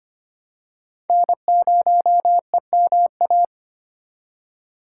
Morse code, heard as a keyed tone, about 25 wpm, 700 Hz. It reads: N0EMA
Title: USA Call Signs - 5 Characters - 4x Character Spacing - 25wpm